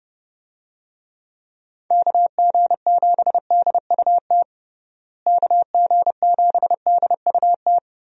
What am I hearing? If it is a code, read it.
Morse code, heard as a keyed tone, about 32 wpm, 700 Hz. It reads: KG7DUT KG7DUT
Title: USA Call Signs - 6 Characters - Repeated 3x - 30wpm